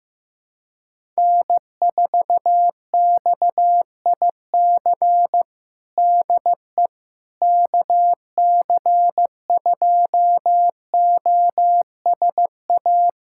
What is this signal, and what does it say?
Morse code, heard as a keyed tone, about 15 wpm, 700 Hz. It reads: N4XIC DE KC2OSA
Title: USA Callsign DE Callsign 15wpm